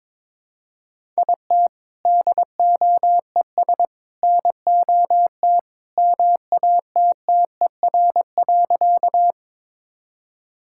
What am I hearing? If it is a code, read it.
Morse code, heard as a keyed tone, about 22 wpm, 700 Hz. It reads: IT DOES NOT MATTER.
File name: practice / 22wpm / Sentences with 3-Word Phrases - 22wpm